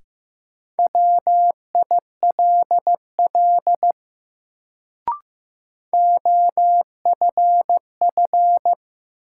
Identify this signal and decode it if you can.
Morse code, heard as a keyed tone, about 15 wpm, 700 Hz. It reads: WILL OFF
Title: Sets of 2 Words - New Word Drill - 15wpm